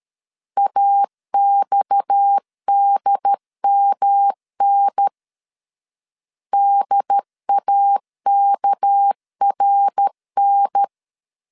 Morse code, heard as a keyed tone, about 13 wpm, 800 Hz. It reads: AXDMN DAKRN
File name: CW / DET / 1 9 1500C S C 12wpm 3811